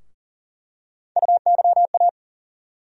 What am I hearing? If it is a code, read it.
Morse code, heard as a keyed tone, about 40 wpm, 700 Hz. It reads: UYA
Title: USA Call Sign Postfixes - 3 Letters 40wpm